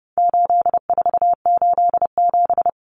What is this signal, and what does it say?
Morse code, heard as a keyed tone, about 30 wpm, 700 Hz. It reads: 8487
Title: Common Cut Numbers - 4-Digit 30wpm